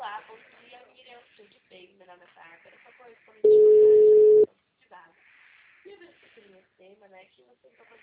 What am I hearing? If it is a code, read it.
Ringback tone (Europe, 425 Hz), with speech.
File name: out- -2007- - - .96